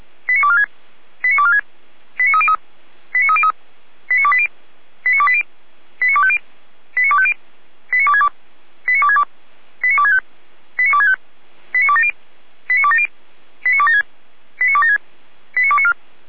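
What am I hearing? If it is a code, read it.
Five-tone selective call (ZVEI), heard as quick runs of five tones, tones of about 70 ms. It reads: 89246 89246 89303 89303 89280 89280 89250 89250 89262 89262 89265 89265 89270 89270 89276 89276 89294
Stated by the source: ZVEI five-tone selective call received by SDR.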